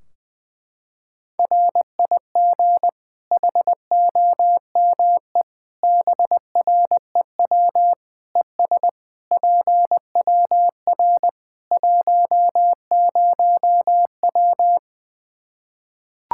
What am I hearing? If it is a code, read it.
Morse code, heard as a keyed tone, about 20 wpm, 700 Hz. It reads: RIG HOME BREW ES PWR 10W
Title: Practice QSO - Exchange 2 - Info - 20wpm